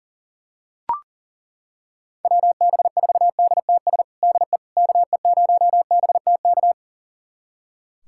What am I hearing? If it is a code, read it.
Morse code, heard as a keyed tone, about 40 wpm, 700 Hz. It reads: WB4DTS DE KE0BTK
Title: USA Callsign DE Callsign - 6 Characters - 40wpm